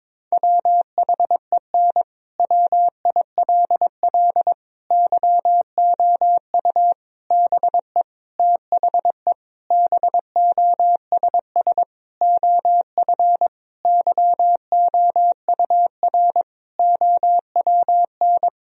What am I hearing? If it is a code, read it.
Morse code, heard as a keyed tone, about 22 wpm, 700 Hz. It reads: WHEN WILL YOU BE THE BOSS OF YOUR OWN
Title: Sentences from Top 1000 Words - Review Part 4 - 22wpm